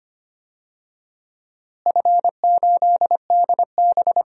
Morse code, heard as a keyed tone, about 25 wpm, 700 Hz. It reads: F8DB
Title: DXCC-100 Call Signs - 4 Characters - Repeated 3x - 25wpm